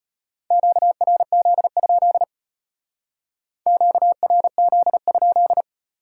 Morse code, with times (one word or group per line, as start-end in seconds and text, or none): 0.5-2.2 QRZ?
3.7-5.6 QRZ?